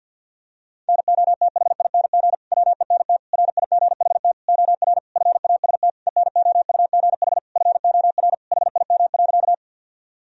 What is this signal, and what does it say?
Morse code, heard as a keyed tone, about 52 wpm, 700 Hz. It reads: NOTHING WENT RIGHT OR FAST ENOUGH FOR HIM.
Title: Sentences from Top 1000 Words - Review Part 2 - 50wpm